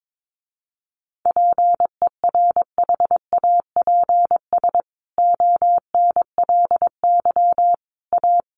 Morse code, heard as a keyed tone, about 22 wpm, 700 Hz. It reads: PERHAPS ONLY A
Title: A Sound Of Thunder 22wpm